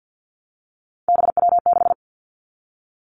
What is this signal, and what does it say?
Morse code, heard as a keyed tone, about 50 wpm, 700 Hz. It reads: BW6
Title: Three Challenging Letter-Number Combinations 50wpm